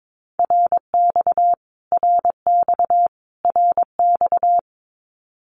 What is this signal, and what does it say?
Morse code, heard as a keyed tone, about 22 wpm, 700 Hz. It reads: RX RX RX
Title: Abbreviations Repeated 3x - 22wpm